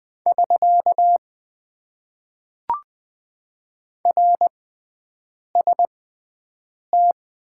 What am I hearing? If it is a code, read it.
Morse code, heard as a keyed tone, about 20 wpm, spaced as for 7 wpm, 700 Hz. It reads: <SK> RST
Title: Top 90 QSO Elements - 6x Character Spacing - 20wpm